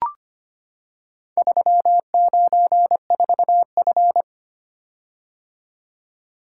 Morse code, heard as a keyed tone, about 25 wpm, 700 Hz. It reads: 394F
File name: practice / 25wpm / Four Letter-Number Combinations - 25wpm